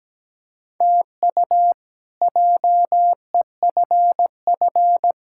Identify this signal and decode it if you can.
Morse code, heard as a keyed tone, about 17 wpm, 700 Hz. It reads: TU JEFF
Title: K1USN SST QSO Practice - 17wpm